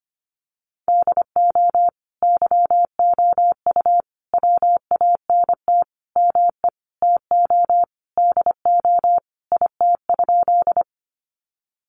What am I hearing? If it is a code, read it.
Morse code, heard as a keyed tone, about 25 wpm, 700 Hz. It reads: DO YOU WANT ME TO DO IT?
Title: Sentences from Top 200 Words 25wpm